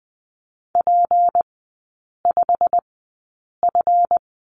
Morse code, heard as a keyed tone, about 20 wpm, spaced as for 8 wpm, 700 Hz. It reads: P5F
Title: 205 Lesson 33e - Letter j - Three Characters 20wpm